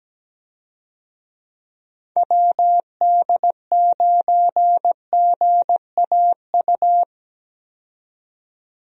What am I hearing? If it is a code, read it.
Morse code, heard as a keyed tone, about 17 wpm, 700 Hz. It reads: WD9GAU
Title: USA Call Signs - 6 Characters - 17wpm